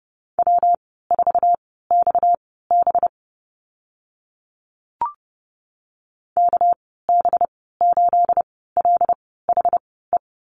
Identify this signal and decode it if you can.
Morse code, heard as a keyed tone, about 30 wpm, spaced as for 17 wpm, 700 Hz. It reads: W4XB KB8LHE
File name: Lesson 37g - Letter x - Call Signs 30wpm